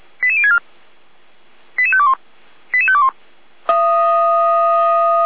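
Five-tone selective call (ZVEI), heard as quick runs of five tones, tones of about 70 ms. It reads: 88064 88521 88521
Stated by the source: SDR-received ZVEI five-tone selective call, code 88063.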